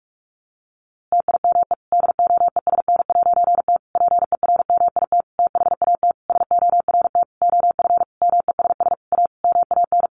Morse code, heard as a keyed tone, about 44 wpm, 700 Hz. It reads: TIME DOESN'T PERMIT THAT SORT OF MESS A MAN
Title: A Sound Of Thunder 45wpm